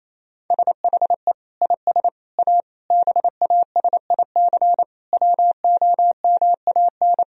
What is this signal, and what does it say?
Morse code, heard as a keyed tone, about 28 wpm, 700 Hz. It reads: SHE IS A BASIC WOMAN
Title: Sentences from Top 1000 Words - Review Part 3 - 28wpm